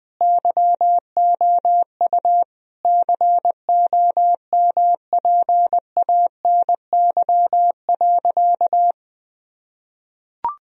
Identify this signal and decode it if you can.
Morse code, heard as a keyed tone, about 21 wpm, 700 Hz. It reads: YOU COMPANY.
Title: Sentences from Top 1000 Words - Review Part 3 - 20wpm